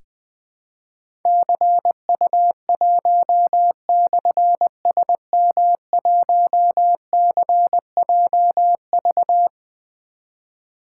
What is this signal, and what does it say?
Morse code, heard as a keyed tone, about 20 wpm, 700 Hz. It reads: CU1/SM1CJV
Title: Difficult Call Signs 20wpm